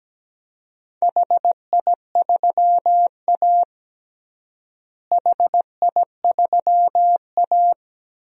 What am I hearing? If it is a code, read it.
Morse code, heard as a keyed tone, about 17 wpm, 700 Hz. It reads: HI3A HI3A
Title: DXCC-100 Call Signs - 4 Characters - Repeated 2x - 17wpm